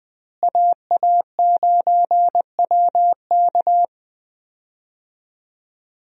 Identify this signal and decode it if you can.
Morse code, heard as a keyed tone, about 21 wpm, 700 Hz. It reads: AA9WK
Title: USA Call Signs - 5 Characters - Repeated 3x - 20wpm